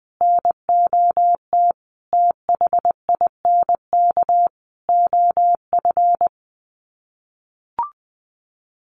Morse code, heard as a keyed tone, about 20 wpm, 700 Hz. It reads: NOT THINK OF
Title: Sets of 5 Words 20wpm